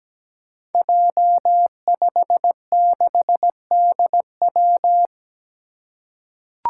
Morse code, heard as a keyed tone, about 17 wpm, 700 Hz. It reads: J56DW
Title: Five Challenging Letter-Number Combinations 17wpm